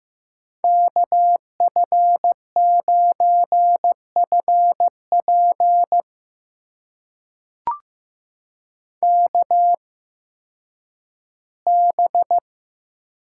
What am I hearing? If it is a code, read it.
Morse code, heard as a keyed tone, about 15 wpm, spaced as for 10 wpm, 700 Hz. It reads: KF9FP KB
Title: USA Call Signs - 8x Character Spacing - 15wpm